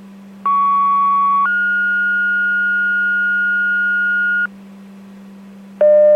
Two-tone page, heard as a one-second tone, then a three-second tone, about 1120 then 1430 Hz, with noise. It is heard from 0.5 to 4.5 s.